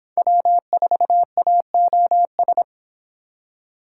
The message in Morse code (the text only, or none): W4AOS